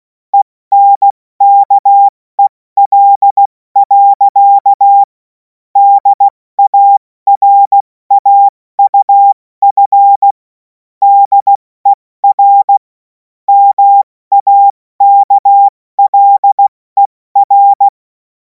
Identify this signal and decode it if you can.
Morse code, heard as a keyed tone, about 16 wpm, 800 Hz. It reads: ENKEL. DARAUF DER MAKLER
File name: Morsekurs / Witze / Witz 30. Juni 2026 70 F80